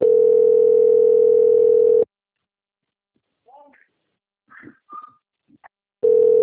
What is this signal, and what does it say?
Ringback tone (North America), 440 + 480 Hz.